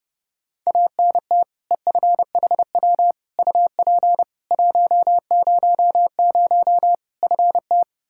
Morse code, heard as a keyed tone, about 30 wpm, 700 Hz. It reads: ANT EFHW UP 100 FT
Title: Practice QSO - Exchange 2 - Info - 30wpm